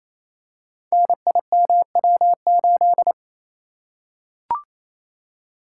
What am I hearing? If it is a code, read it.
Morse code, heard as a keyed tone, about 27 wpm, 700 Hz. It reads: NIMW8
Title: Code Groups - 5 Letter:Numbers - 28wpm